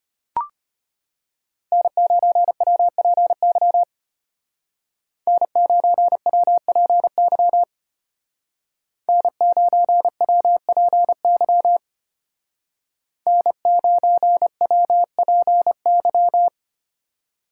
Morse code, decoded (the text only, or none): N9WPY N9WPY N9WPY N9WPY